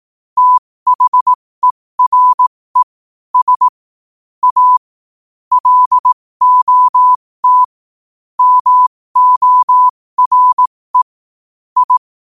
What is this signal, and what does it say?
Morse code, heard as a keyed tone, about 18 wpm, spaced as for 15 wpm, 1000 Hz. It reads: THERE S A LOT MORE I
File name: qotdfiles / 15wpm